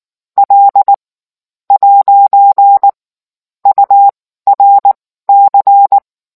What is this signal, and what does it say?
Morse code, heard as a keyed tone, about 19 wpm, spaced as for 11 wpm, 800 Hz. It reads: L'URC